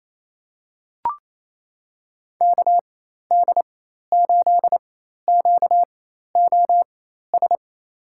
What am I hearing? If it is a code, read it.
Morse code, heard as a keyed tone, about 28 wpm, spaced as for 13 wpm, 700 Hz. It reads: KD8QOS
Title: USA Call Signs - 6 Characters - 4x Character Spacing - 28wpm